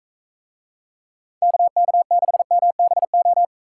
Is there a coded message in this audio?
Yes: Morse code from 1.4 to 3.5 s, heard as a keyed tone, about 43 wpm, 700 Hz. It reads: KK6MBO